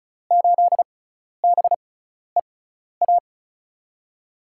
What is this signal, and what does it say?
Morse code, heard as a keyed tone, about 34 wpm, spaced as for 12 wpm, 700 Hz. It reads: 8BEA
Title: USA Call Signs - 6 Characters - 6x Character Spacing - 35wpm